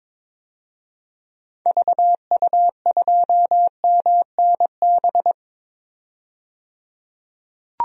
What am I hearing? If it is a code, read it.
Morse code, heard as a keyed tone, about 22 wpm, 700 Hz. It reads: VU2MNB
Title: DXCC-100 Call Signs - 6 Characters - 22wpm